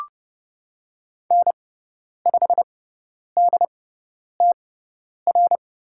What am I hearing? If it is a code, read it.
Morse code, heard as a keyed tone, about 30 wpm, spaced as for 10 wpm, 700 Hz. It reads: N5DTR
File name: Lesson 17g - Letter u - Call Signs 30wpm